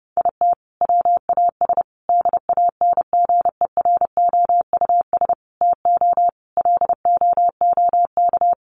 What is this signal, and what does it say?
Morse code, heard as a keyed tone, about 28 wpm, 700 Hz. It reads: IT WAS DANGEROUS TO LOOK
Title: Sentences from Top 1300 Words 30wpm